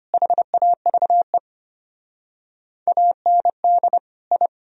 Morse code, heard as a keyed tone, about 30 wpm, 700 Hz. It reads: HAVE AND I
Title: Speed-Racing - Sets of 3 Words 25wpm